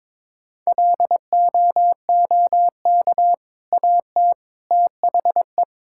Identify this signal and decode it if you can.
Morse code, heard as a keyed tone, about 22 wpm, 690 Hz. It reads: LOOK AT THE
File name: practice / 22wpm / Easy Sentences with Hints - 22wpm